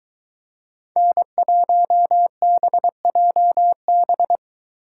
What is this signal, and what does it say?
Morse code, heard as a keyed tone, about 23 wpm, 700 Hz. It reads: N1BJB